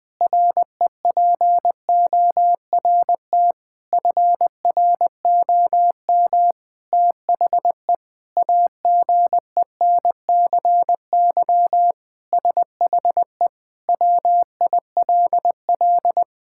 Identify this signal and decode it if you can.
Morse code, heard as a keyed tone, about 20 wpm, 700 Hz. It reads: REPORT FROM THE AGENCY SHE WILL
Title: Sentences with 7-Word Phrases - 20wpm